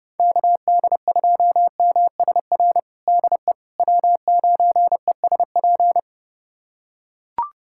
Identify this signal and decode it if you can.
Morse code, heard as a keyed tone, about 30 wpm, 700 Hz. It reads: KD2MSR DE W9ESP
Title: USA Callsign DE Callsign 30wpm